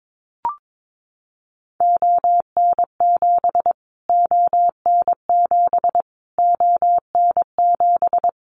Morse code, heard as a keyed tone, about 22 wpm, 700 Hz. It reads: ON7 ON7 ON7
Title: Callsign Prefixes for Top 25 DXCC - Repeated 3x - 22wpm